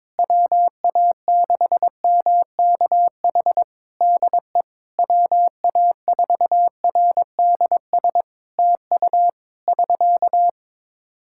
Morse code, heard as a keyed tone, about 22 wpm, 700 Hz. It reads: WA6MKH DE WA4RDS TU <SK>
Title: Practice QSO - Exchange 4 - Ending - 22wpm